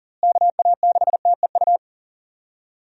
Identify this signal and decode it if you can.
Morse code, heard as a keyed tone, about 40 wpm, 700 Hz. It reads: KA6TEU
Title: USA Call Signs - 6 Characters - Repeated 2x - 40wpm